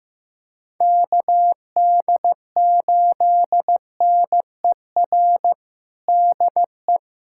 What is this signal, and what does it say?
Morse code, heard as a keyed tone, about 16 wpm, 700 Hz. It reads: KD8NER DE
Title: USA Callsign DE Callsign - 6 Characters - 15wpm